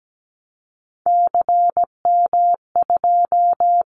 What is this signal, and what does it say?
Morse code, heard as a keyed tone, about 17 wpm, 700 Hz. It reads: CM2
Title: Callsign Prefixes for Top 100 DXCC - 17wpm